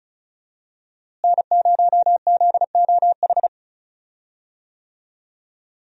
Morse code, heard as a keyed tone, about 34 wpm, 700 Hz. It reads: N0ZOH
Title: USA Call Signs - 5 Characters - Repeated 2x - 35wpm